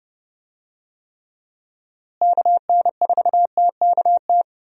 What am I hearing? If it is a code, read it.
Morse code, heard as a keyed tone, about 30 wpm, 700 Hz. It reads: KN4TKT
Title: USA Call Signs - 6 Characters - Repeated 3x - 30wpm